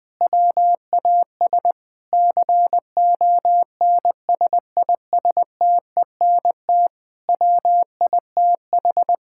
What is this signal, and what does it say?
Morse code, heard as a keyed tone, about 20 wpm, 700 Hz. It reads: WAS CONSISTENT WITH
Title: Sentences from Top 1600 Words 20wpm